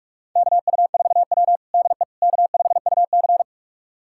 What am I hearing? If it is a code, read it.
Morse code, heard as a keyed tone, about 42 wpm, 700 Hz. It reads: KU4W DE K5UC
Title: USA Callsign DE Callsign - 4 Characters - 45wpm